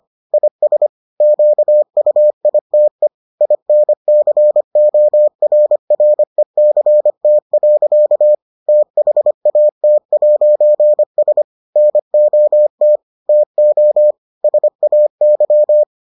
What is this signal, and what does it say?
Morse code, heard as a keyed tone, about 25 wpm, 600 Hz. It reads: IS QUITE INCORRECT. THAT'S NOT TO SAY